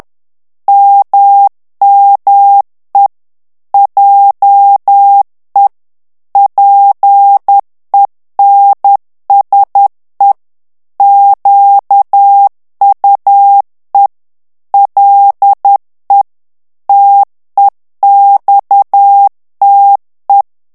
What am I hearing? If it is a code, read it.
Morse code, heard as a keyed tone, about 10 wpm, 800 Hz. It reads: MME JE PENSE QUE LE TEXTE
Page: Français - texte en clair 1 - vitesse : 10 mots minute